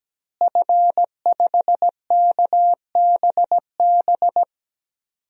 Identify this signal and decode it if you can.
Morse code, heard as a keyed tone, about 17 wpm, 700 Hz. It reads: F5KBB
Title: DXCC-100 Call Signs - 5 Characters - Repeated 3x - 17wpm